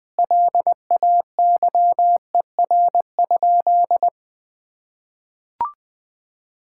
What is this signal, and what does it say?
Morse code, heard as a keyed tone, about 20 wpm, 700 Hz. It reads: LAYER?
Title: Sentences from Top 1600 Words 20wpm